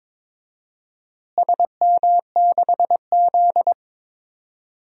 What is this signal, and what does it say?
Morse code, heard as a keyed tone, about 22 wpm, 700 Hz. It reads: SM6Z
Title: DXCC-100 Call Signs - 4 Characters - Repeated 2x - 22wpm